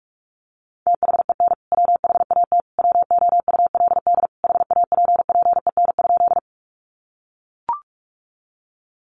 Morse code, heard as a keyed tone, about 44 wpm, 700 Hz. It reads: THEN WHAT WOULD HAPPEN?
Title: Sentences from Top 400 Words 45wpm